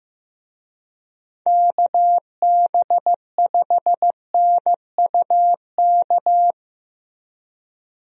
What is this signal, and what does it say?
Morse code, heard as a keyed tone, about 15 wpm, 700 Hz. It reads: KB5NUK
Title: USA Call Signs - 6 Characters - Repeated 2x - 15wpm